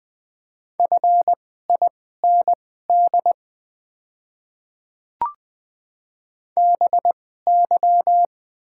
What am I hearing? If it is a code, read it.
Morse code, heard as a keyed tone, about 20 wpm, spaced as for 15 wpm, 700 Hz. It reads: FIND BY
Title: 2x Character Spacing - Top 200 Words - 20wpm